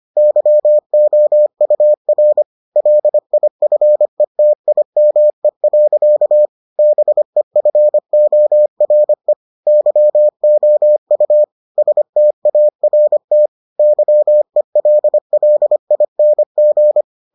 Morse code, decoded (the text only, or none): YOUR LIFETIME. BEFORE YOU START YELLING